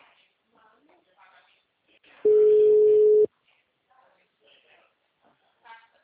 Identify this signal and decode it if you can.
Ringback tone (Europe, 425 Hz), with speech.